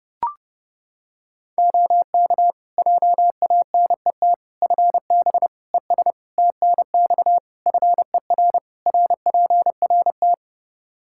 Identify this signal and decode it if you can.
Morse code, heard as a keyed tone, about 30 wpm, 700 Hz. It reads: OK JANET FB ES TNX FER RPRT
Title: Practice QSO - Exchange 2 - Info - 30wpm